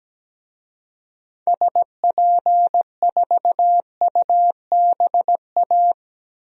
Morse code, heard as a keyed tone, about 17 wpm, 700 Hz. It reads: SP4UBA
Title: DXCC-100 Call Signs - 6 Characters - Repeated 2x - 17wpm